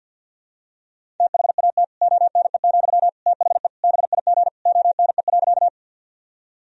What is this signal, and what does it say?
Morse code, heard as a keyed tone, about 46 wpm, 700 Hz. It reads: THAT ONE, THE BIG ONE.
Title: Sentences from Top 1000 Words - Review Part 1 - 50wpm